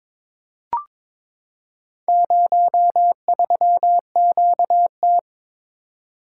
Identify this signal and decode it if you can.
Morse code, heard as a keyed tone, about 22 wpm, 700 Hz. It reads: 03QT
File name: practice / 22wpm / Four Letter-Number Combinations - 22wpm